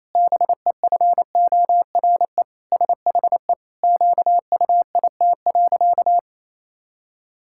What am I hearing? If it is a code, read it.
Morse code, heard as a keyed tone, about 28 wpm, 700 Hz. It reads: BEFORE SHE QUIT.
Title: Pangram Challenge v2 - 28wpm